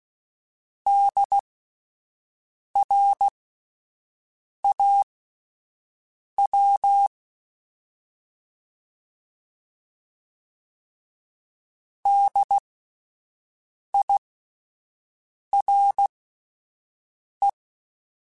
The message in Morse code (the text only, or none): DRAW DIRE